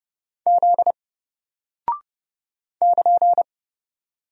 Morse code, heard as a keyed tone, about 30 wpm, 700 Hz. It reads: Z <KN>
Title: Single Character (Letter:Number:Punctuation:Prosign) Alt 30wpm